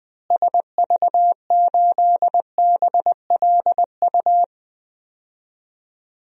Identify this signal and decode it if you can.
Morse code, heard as a keyed tone, about 20 wpm, 690 Hz. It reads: SV8BLU
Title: DXCC-100 Call Signs - 6 Characters - Repeated 2x - 20wpm